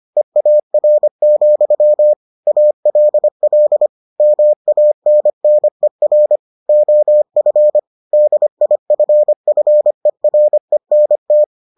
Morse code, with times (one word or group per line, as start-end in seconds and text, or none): 0.2-2.1 EAR,
2.5-3.9 ALL
4.2-6.4 MANNER
6.7-7.8 OF
8.1-11.4 DIFFERENT